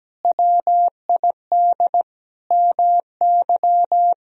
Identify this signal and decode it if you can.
Morse code, heard as a keyed tone, about 17 wpm, 700 Hz. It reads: WID MY
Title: Sentences from Top 300 Words and Abbreviations 17wpm